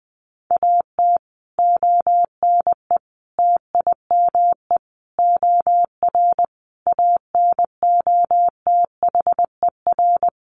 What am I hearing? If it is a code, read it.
Morse code, heard as a keyed tone, about 20 wpm, 700 Hz. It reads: AT ONE TIME OR ANOTHER
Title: Sets of 5 Words Encore 20wpm